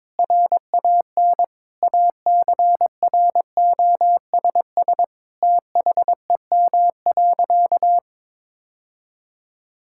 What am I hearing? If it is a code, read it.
Morse code, heard as a keyed tone, about 22 wpm, 700 Hz. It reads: RAN ACROSS THEM.
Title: Sentences from Top 1000 Words - Review Part 5 - 22wpm